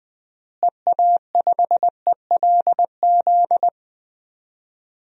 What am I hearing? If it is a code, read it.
Morse code, heard as a keyed tone, about 20 wpm, 700 Hz. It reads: EA5ELZ
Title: DXCC-100 Call Signs - 6 Characters - Repeated 3x - 20wpm